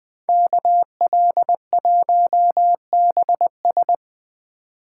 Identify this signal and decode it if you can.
Morse code, heard as a keyed tone, about 21 wpm, 700 Hz. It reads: KL1BS